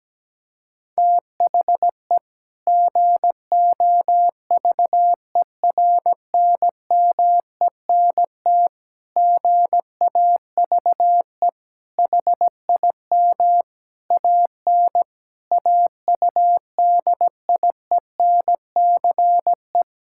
Morse code, heard as a keyed tone, about 17 wpm, 700 Hz. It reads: THE GOVERNMENT GAVE HIM AN AUDIENCE
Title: Sentences from Top 1000 Words - Review Part 5 - 17wpm